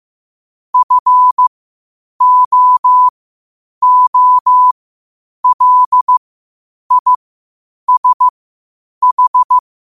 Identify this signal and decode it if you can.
Morse code, heard as a keyed tone, about 15 wpm, spaced as for 8 wpm, 1000 Hz. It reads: FOOLISH
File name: qotdfiles / 8wpm